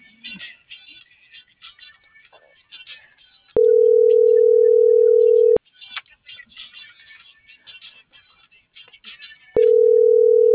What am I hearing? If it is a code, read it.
Ringback tone (North America), 440 + 480 Hz.